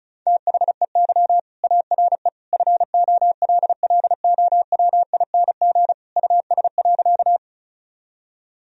Morse code, heard as a keyed tone, about 34 wpm, 700 Hz. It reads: THEY ARE FOLLOWING US.
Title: Sentences from Top 1000 Words - Review Part 4 - 35wpm